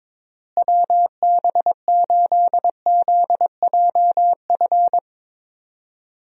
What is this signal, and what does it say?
Morse code, heard as a keyed tone, about 22 wpm, 700 Hz. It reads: WB8ZJF
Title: USA Call Signs - 6 Characters - Repeated 3x - 22wpm